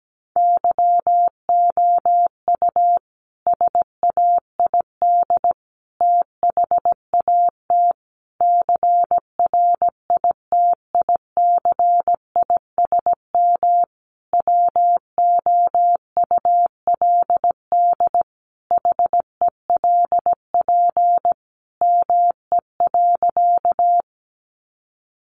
Morse code, heard as a keyed tone, about 17 wpm, 700 Hz. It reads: YOU SAID THAT CRITICISM WOULD HELP ME.
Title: Sentences from Top 1300 Words 17wpm